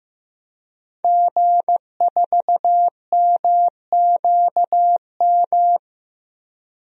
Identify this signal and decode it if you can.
Morse code, heard as a keyed tone, about 16 wpm, 700 Hz. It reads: G4MQM
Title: DXCC-100 Call Signs - 5 Characters - Repeated 3x - 15wpm